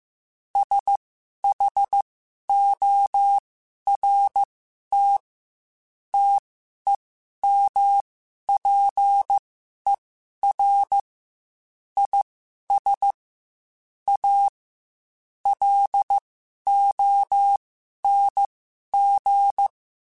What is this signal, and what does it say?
Morse code, heard as a keyed tone, about 15 wpm, spaced as for 11 wpm, 800 Hz. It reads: SHORT TEMPER IS A LONG
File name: Quotes / Quotes-mp3s / Temper @13